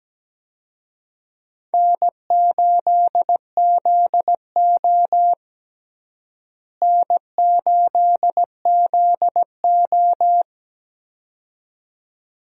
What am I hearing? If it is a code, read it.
Morse code, heard as a keyed tone, about 17 wpm, 700 Hz. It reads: N8ZO N8ZO
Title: USA Call Signs - 4 Characters - Repeated 2x - 17wpm